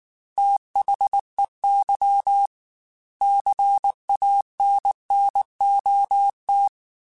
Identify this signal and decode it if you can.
Morse code, heard as a keyed tone, about 19 wpm, 800 Hz. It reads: THEY CANNOT
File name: Quotes / Quotes-mp3s / Advice @18